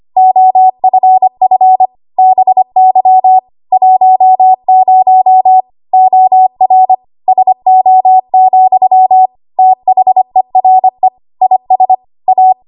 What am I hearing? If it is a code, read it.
Morse code, heard as a keyed tone, about 25 wpm, 750 Hz. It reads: OFF BY 10 OR SO, THERE IS A
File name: Archive / 25 WPM / 25WPM